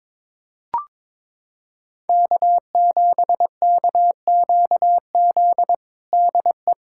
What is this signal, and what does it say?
Morse code, heard as a keyed tone, about 22 wpm, 700 Hz. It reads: K7KQZ DE
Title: USA Callsign DE Callsign 22wpm